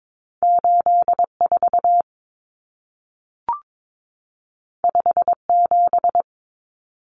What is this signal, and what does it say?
Morse code, heard as a keyed tone, about 22 wpm, 700 Hz. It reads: 84 57
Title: Common Cut Numbers - 2-Digit 22wpm